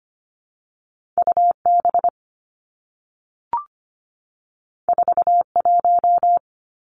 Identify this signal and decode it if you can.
Morse code, heard as a keyed tone, about 25 wpm, 700 Hz. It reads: UB 41
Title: Two Challenging Letter-Number Combinations 25wpm